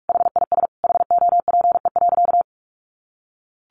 Morse code, heard as a keyed tone, about 44 wpm, 700 Hz. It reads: HIS HOPE.
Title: Sentences from Top 1100 Words 45wpm